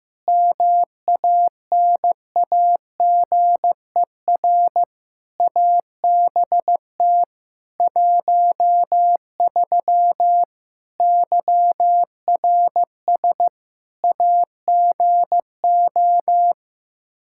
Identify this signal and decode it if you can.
Morse code, heard as a keyed tone, about 15 wpm, 700 Hz. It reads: MANAGER ABT 13 YRS AGO
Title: K1USN SST QSO Practice 15wpm